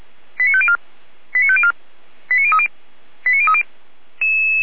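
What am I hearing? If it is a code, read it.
Five-tone selective call (ZVEI), heard as quick runs of five tones, tones of about 70 ms. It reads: 89504 89504 89030 89030
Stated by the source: ZVEI five-tone selective call received on SDR.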